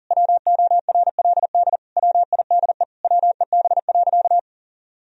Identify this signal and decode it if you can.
Morse code, heard as a keyed tone, about 40 wpm, 700 Hz. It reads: WORLD WIDE WEB.
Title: Sentences from Top 1000 Words - Review Part 4 - 40wpm